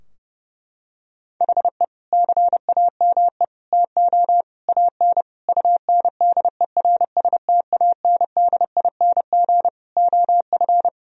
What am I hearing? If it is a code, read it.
Morse code, heard as a keyed tone, about 30 wpm, 700 Hz. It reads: HE CAME TO AN UNDERSTANDING OF
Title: Sentences from Top 1000 Words - Review Part 3 - 30wpm